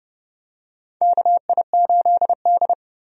Morse code, heard as a keyed tone, about 32 wpm, 700 Hz. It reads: KI8D
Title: USA Call Signs - 4 Characters - 30wpm